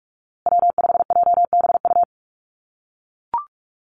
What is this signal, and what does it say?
Morse code, heard as a keyed tone, about 44 wpm, 700 Hz. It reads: W5JBU
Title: USA Call Signs Repeated 3x 45wpm